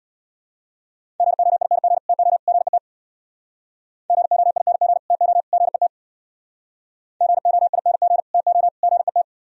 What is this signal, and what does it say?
Morse code, heard as a keyed tone, about 63 wpm, 700 Hz. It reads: GOING TO GET GOING TO GET GOING TO GET